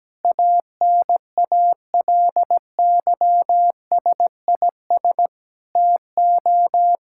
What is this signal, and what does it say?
Morse code, heard as a keyed tone, about 17 wpm, 700 Hz. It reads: ANALYSIS TO
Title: Sentences from Top 1000 Words - Review Part 5 - 17wpm